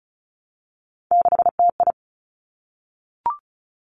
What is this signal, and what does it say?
Morse code, heard as a keyed tone, about 35 wpm, 700 Hz. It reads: 6TI
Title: Three Challenging Letter-Number Combinations 35wpm